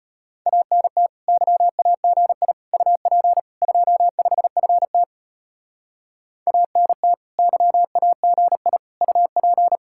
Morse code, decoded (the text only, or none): ANT YAGI UP 25FT ANT YAGI UP